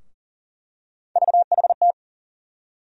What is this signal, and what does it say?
Morse code, heard as a keyed tone, about 40 wpm, 700 Hz. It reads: VHT
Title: USA Call Sign Postfixes - 3 Letters 40wpm